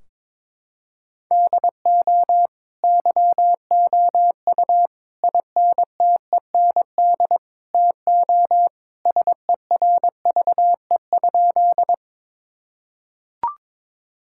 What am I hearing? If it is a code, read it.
Morse code, heard as a keyed tone, about 22 wpm, 700 Hz. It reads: DO YOU INTEND TO SERVE?
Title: Sentences from Top 1600 Words 22wpm